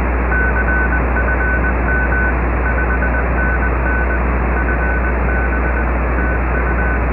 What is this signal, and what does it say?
Morse code, heard as a keyed tone, about 20 wpm, 1500 Hz. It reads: CFM 5NN FB TU